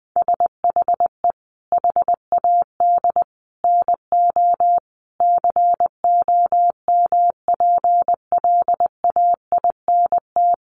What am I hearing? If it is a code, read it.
Morse code, heard as a keyed tone, about 20 wpm, 700 Hz. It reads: SHE HAD NO COMPLAINT
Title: Sentences from Top 1400 Words 20wpm